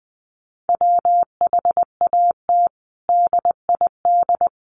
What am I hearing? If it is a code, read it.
Morse code, heard as a keyed tone, about 20 wpm, 700 Hz. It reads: WHAT DID
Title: Sentences from Top 1400 Words 20wpm